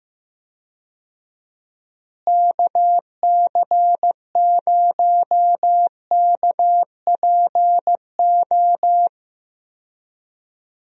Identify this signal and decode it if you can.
Morse code, heard as a keyed tone, about 16 wpm, 700 Hz. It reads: KC0KPO
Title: USA Call Signs - 6 Characters - Repeated 3x - 15wpm